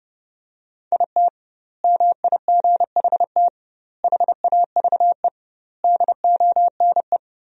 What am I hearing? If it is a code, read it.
Morse code, heard as a keyed tone, about 30 wpm, 700 Hz. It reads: IT MIGHT HAVE DONE
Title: 2x Word-Spacing - Sentences from Top 500 Words - 30wpm